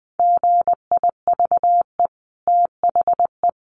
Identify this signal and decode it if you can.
Morse code, heard as a keyed tone, about 20 wpm, 700 Hz. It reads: GIVE THE
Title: Sentences from Top 1500 Words 20wpm